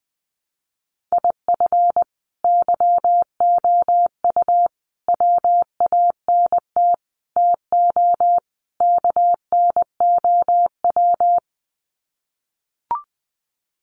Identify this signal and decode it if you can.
Morse code, heard as a keyed tone, about 20 wpm, 700 Hz. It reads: IF YOU WANT TO KNOW
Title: Sets of 5 Words Encore2 20wpm